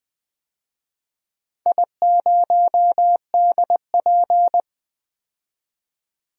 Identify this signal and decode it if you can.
Morse code, heard as a keyed tone, about 20 wpm, 700 Hz. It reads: I0DP
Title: DXCC-100 Call Signs - 4 Characters - 20wpm